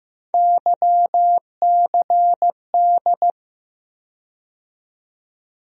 Morse code, heard as a keyed tone, about 15 wpm, 700 Hz. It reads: YCD
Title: USA Call Sign Postfixes - 3 Letters 15wpm